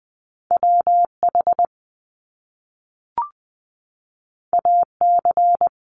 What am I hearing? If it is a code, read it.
Morse code, heard as a keyed tone, about 20 wpm, 700 Hz. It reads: W H A C
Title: Top Two Letter Combinations in Words 20wpm